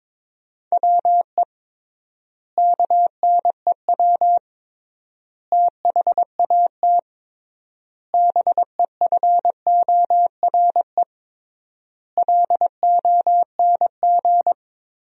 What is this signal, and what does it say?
Morse code, heard as a keyed tone, about 22 wpm, 700 Hz. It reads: WE KNEW THAT BEFORE LONG
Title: 3x Word-Spacing - Sentences from Top 400 Words - 22wpm